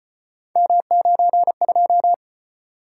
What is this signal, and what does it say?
Morse code, heard as a keyed tone, about 35 wpm, 700 Hz. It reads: M92